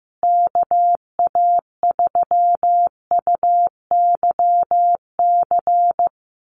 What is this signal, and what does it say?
Morse code, heard as a keyed tone, about 15 wpm, 700 Hz. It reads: KA3UYC
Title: USA Call Signs Repeated 2x 15wpm